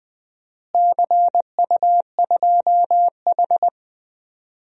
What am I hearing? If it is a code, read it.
Morse code, heard as a keyed tone, about 20 wpm, 700 Hz. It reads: CU2H
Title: DXCC-100 Call Signs - 4 Characters - Repeated 3x - 20wpm